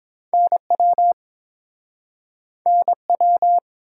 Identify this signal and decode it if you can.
Morse code, heard as a keyed tone, about 30 wpm, 700 Hz. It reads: NW NW